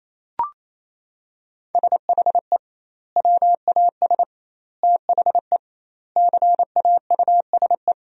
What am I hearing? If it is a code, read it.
Morse code, heard as a keyed tone, about 28 wpm, 700 Hz. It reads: SHE WAS THE CAUSE
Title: 2x Word-Spacing - Sentences from Top 200 Words - 28wpm